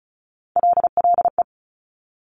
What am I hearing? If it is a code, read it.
Morse code, heard as a keyed tone, about 36 wpm, 700 Hz. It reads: LLE
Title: Top Three Letter Combinations in Words 35wpm